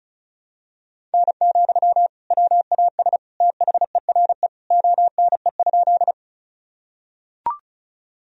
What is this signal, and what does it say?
Morse code, heard as a keyed tone, about 36 wpm, 700 Hz. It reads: N, WAS THERE ONE?
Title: Sentences from Top 100 Words and Abbreviations 35wpm